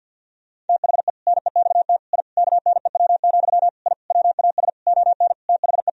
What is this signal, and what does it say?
Morse code, heard as a keyed tone, about 50 wpm, 700 Hz. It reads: THE NEXT I KNEW, I WAS ON THE
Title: Sentences from Top 1000 Words - Review Part 3 - 50wpm